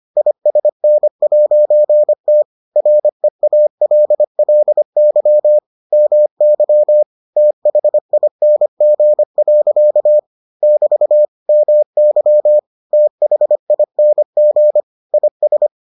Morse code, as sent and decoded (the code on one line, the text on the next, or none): .. ... -. .----. - / .-. .- .-.. .-.. -.-- / -- -.-- / - .... .. -. --. .-.-.- / -...- / -- -.-- / - .... .. -. --. / .. ...
ISN'T REALLY MY THING. <BT> MY THING IS